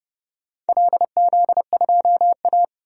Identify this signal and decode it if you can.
Morse code, heard as a keyed tone, about 30 wpm, 700 Hz. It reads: LZ2A
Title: DXCC-100 Call Signs - 4 Characters - 30wpm